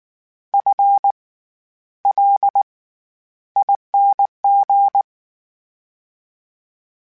Morse code, heard as a keyed tone, about 19 wpm, spaced as for 11 wpm, 800 Hz. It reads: FLING
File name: Special INGs / Special INGs-mp3s / SlowToFaster ing @18